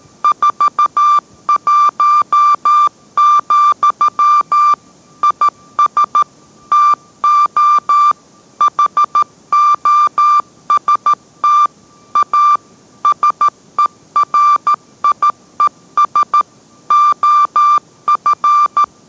Morse code, heard as a keyed tone, about 13 wpm, 1180 Hz. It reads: 41, IS TO HOST A SERIES OF